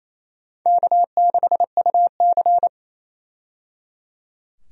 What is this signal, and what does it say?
Morse code, heard as a keyed tone, about 28 wpm, 700 Hz. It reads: K6UC
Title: USA Call Signs - 4 Characters - 28wpm